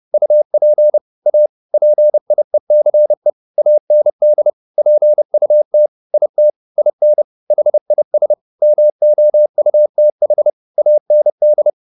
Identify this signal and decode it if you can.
Morse code, heard as a keyed tone, about 30 wpm, 600 Hz. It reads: UP A PIECE AND PUT IT IN HIS MOUTH AND
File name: the old man and the sea / the old man and the sea 1 30wpm